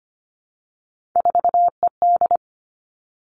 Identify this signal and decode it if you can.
Morse code, heard as a keyed tone, about 25 wpm, 700 Hz. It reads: 4ED
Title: Three Challenging Letter-Number Combinations 25wpm